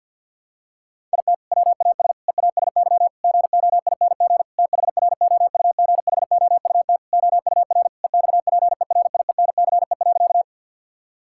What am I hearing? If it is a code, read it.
Morse code, heard as a keyed tone, about 52 wpm, 700 Hz. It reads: IT WAS EASY GOING THROUGHOUT OUR EXPERIENCE.
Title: Sentences from Top 1000 Words - Review Part 5 - 50wpm